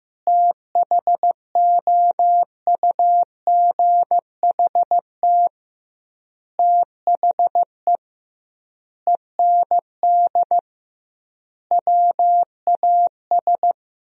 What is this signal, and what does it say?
Morse code, heard as a keyed tone, about 15 wpm, 700 Hz. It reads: THOUGHT THE END WAS
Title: 2x Word-Spacing - Sentences from Top 300 Words - 15wpm